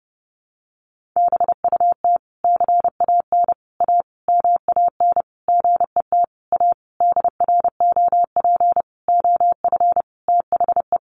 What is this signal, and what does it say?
Morse code, heard as a keyed tone, about 30 wpm, 700 Hz. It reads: BUT CAN A MAN GET A DROP OF THE
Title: Sentences from Top 1400 Words 30wpm